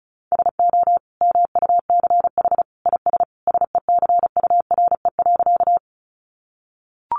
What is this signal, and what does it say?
Morse code, heard as a keyed tone, about 34 wpm, 700 Hz. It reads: SO MUCH IS SECURE.
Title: Sentences from Top 1200 Words 35wpm